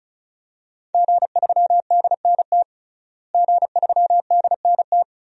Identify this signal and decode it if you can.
Morse code, heard as a keyed tone, about 36 wpm, 700 Hz. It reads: G3DNT G3DNT
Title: DXCC-100 Call Signs - 5 Characters - Repeated 3x - 35wpm